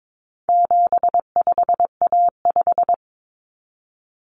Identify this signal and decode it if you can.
Morse code, heard as a keyed tone, about 22 wpm, 700 Hz. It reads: 75A5
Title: Common Cut Numbers - 4-Digit 22wpm